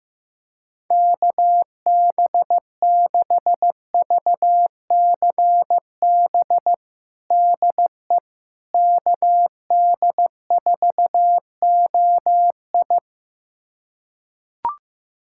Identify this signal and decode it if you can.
Morse code, heard as a keyed tone, about 15 wpm, 700 Hz. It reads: KB6VCB DE KD4OI
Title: USA Callsign DE Callsign 15wpm